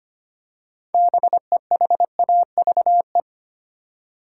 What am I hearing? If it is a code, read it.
Morse code, heard as a keyed tone, about 25 wpm, 700 Hz. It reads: BEHAVE
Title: Six Letter Words - 25wpm